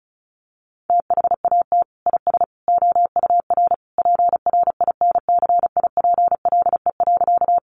Morse code, heard as a keyed tone, about 34 wpm, 700 Hz. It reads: THAT IS OUR PRINCIPLE.
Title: Sentences from Top 1400 Words 35wpm